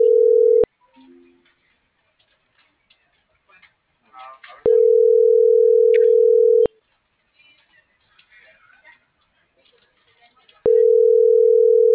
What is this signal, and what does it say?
Ringback tone (North America), 440 + 480 Hz.